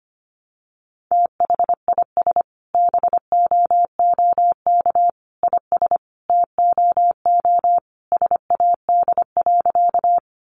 Morse code, heard as a keyed tone, about 25 wpm, 700 Hz. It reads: THIS BOOK IS TOO SAD.
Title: Additional Easy Phrases 25wpm